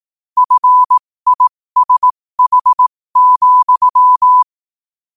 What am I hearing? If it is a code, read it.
Morse code, heard as a keyed tone, about 18 wpm, spaced as for 15 wpm, 1000 Hz. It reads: FISH,
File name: qotdfiles / 15wpm